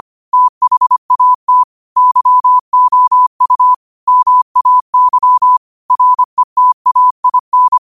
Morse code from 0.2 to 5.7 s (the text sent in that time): THAT YOU MAY